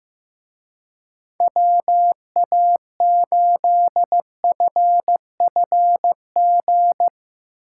Morse code, heard as a keyed tone, about 16 wpm, 700 Hz. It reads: WA8FFG
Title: Speed Racing - USA Call Signs 15wpm